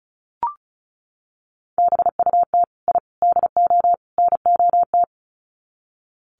Morse code, heard as a keyed tone, about 34 wpm, 700 Hz. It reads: BUT I DO NOT
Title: Sets of 4 Words 35wpm